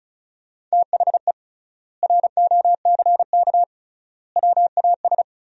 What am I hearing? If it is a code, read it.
Morse code, heard as a keyed tone, about 36 wpm, 700 Hz. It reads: THE ROCK WAS
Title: 3x Word-Spacing - Sentences from Top 400 Words - 35wpm